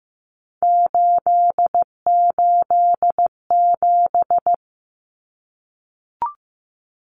Morse code, heard as a keyed tone, about 15 wpm, 700 Hz. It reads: 887
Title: Numbers - 3-Digit 15wpm